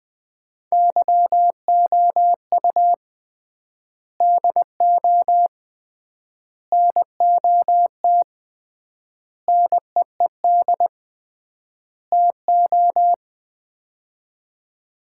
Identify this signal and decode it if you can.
Morse code, heard as a keyed tone, about 20 wpm, 700 Hz. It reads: YOU DO NOT NEED TO
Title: 3x Word-Spacing - Sets of 5 Words - 20wpm